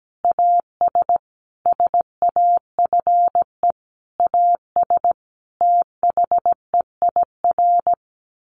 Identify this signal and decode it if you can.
Morse code, heard as a keyed tone, about 17 wpm, 700 Hz. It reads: AS SAFE AS THEIR
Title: Sentences from Top 1500 Words 17wpm